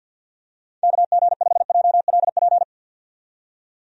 Morse code, heard as a keyed tone, about 51 wpm, 700 Hz. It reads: KG5JLP
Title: USA Call Signs - 6 Characters - 50wpm